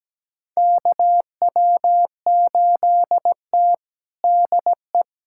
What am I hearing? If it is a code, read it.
Morse code, heard as a keyed tone, about 17 wpm, 700 Hz. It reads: KW8T DE
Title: USA Callsign DE Callsign - 4 Characters - 17wpm